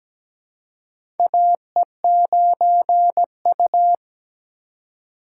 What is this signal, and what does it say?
Morse code, heard as a keyed tone, about 17 wpm, 700 Hz. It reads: AE9U
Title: USA Call Signs - 4 Characters - 4x Character Spacing - 17wpm